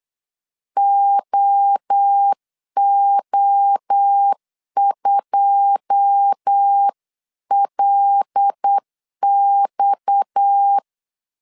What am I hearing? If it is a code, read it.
Morse code, heard as a keyed tone, about 8 wpm, 800 Hz. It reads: OO2LX
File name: LEZIONI / 9 2 8wpm 3437